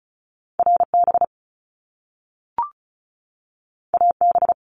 Morse code, heard as a keyed tone, about 36 wpm, 700 Hz. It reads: RB AB
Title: Top Two Letter Combinations in Words 35wpm